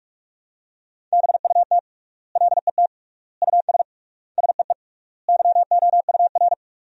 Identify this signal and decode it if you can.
Morse code, heard as a keyed tone, about 44 wpm, 700 Hz. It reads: BUT LET US SEE YOUR
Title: 3x Word-Spacing - Sentences from Top 500 Words - 45wpm